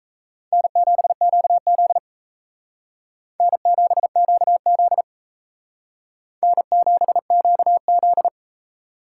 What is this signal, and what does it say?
Morse code, heard as a keyed tone, about 36 wpm, 700 Hz. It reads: N7QZ N7QZ N7QZ